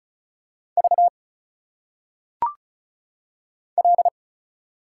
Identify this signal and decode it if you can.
Morse code, heard as a keyed tone, about 35 wpm, 700 Hz. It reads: V L
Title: Single Letters - Phonetic Pronunciation 35wpm